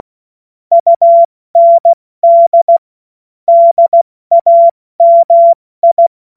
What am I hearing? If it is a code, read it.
Morse code, heard as a keyed tone, about 16 wpm, 680 Hz. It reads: UND DAMI
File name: Morsekurs / Witze / Witz 07. Dezember 2021, 70 F80